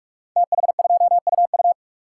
Morse code, heard as a keyed tone, about 44 wpm, 700 Hz. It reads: TH2UU
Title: Code Groups - 5 Letter:Numbers - 45wpm